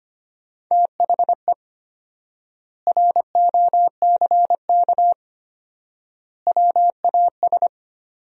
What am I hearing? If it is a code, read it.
Morse code, heard as a keyed tone, about 25 wpm, 700 Hz. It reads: THE ROCK WAS
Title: 4x Word-Spacing - Sentences from Top 400 Words - 25wpm